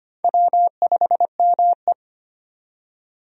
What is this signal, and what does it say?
Morse code, heard as a keyed tone, about 25 wpm, 700 Hz. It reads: W5ME
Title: USA Call Signs - 4 Characters - 25wpm